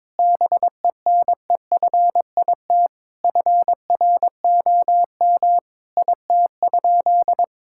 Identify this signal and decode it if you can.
Morse code, heard as a keyed tone, about 22 wpm, 700 Hz. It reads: BENEFIT FROM IT?
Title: Sentences from Top 1000 Words - Review Part 6 - 22wpm